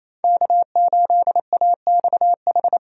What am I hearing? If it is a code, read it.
Morse code, heard as a keyed tone, about 28 wpm, 700 Hz. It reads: K8AXH
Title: USA Call Signs - 5 Characters - 28wpm